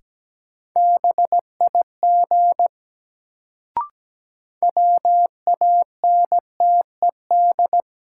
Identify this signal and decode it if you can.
Morse code, heard as a keyed tone, about 17 wpm, 700 Hz. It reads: BIG WANTED
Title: Sets of 2 Words - New Word Drill - 17wpm